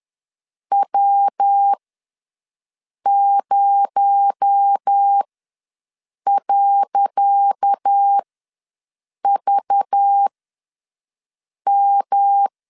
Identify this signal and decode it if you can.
Morse code, heard as a keyed tone, about 10 wpm, spaced as for 5 wpm, 800 Hz. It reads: W0.VM
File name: LEZIONI / 10 SIGNS / 10C DOT ws 10wpm 4213